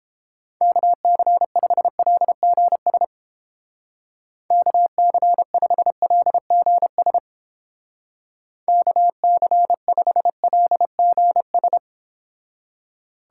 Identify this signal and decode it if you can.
Morse code, heard as a keyed tone, about 29 wpm, 700 Hz. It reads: KC5LGS KC5LGS KC5LGS